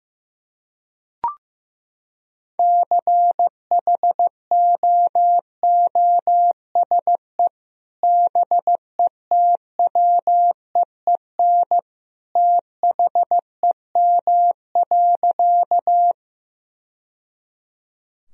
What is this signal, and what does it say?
Morse code, heard as a keyed tone, about 16 wpm, 700 Hz. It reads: CHOOSE BETWEEN THEM.
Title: Sentences from Top 1000 Words - Review Part 4 - 15wpm